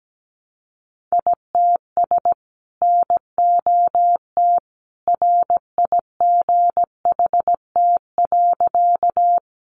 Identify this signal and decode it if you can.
Morse code, heard as a keyed tone, about 17 wpm, 700 Hz. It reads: ITS NOT RIGHT.
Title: Sentences from Top 600 Words 17wpm